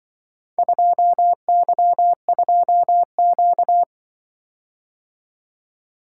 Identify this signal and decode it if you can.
Morse code, heard as a keyed tone, about 24 wpm, 700 Hz. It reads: 2Y2Q
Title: Speed-Racing - Four Letter-Number Combinations - 20wpm